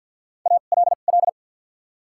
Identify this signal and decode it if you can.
Morse code, heard as a keyed tone, about 49 wpm, spaced as for 35 wpm, 700 Hz. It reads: ALL
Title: 2x Character Spacing - Top 100 Words - 50wpm